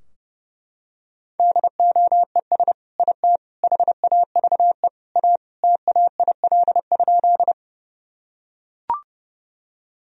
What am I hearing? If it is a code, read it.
Morse code, heard as a keyed tone, about 30 wpm, 700 Hz. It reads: DOES IT HAVE A TAIL?
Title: Sentences from Top 1000 Words - Review Part 3 - 30wpm